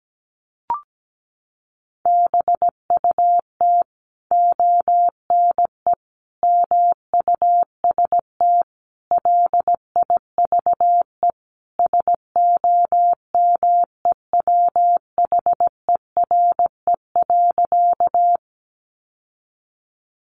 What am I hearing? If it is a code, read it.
Morse code, heard as a keyed tone, about 17 wpm, 700 Hz. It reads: BUT ONE MUST LIVE SOMEWHERE.
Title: Sentences from Top 1400 Words 17wpm